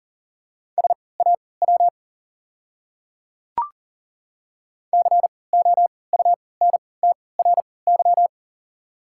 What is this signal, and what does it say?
Morse code, heard as a keyed tone, about 40 wpm, spaced as for 23 wpm, 690 Hz. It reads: SAW COUNTRY
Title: 3x Character Spacing - Top 300 Words - 40wpm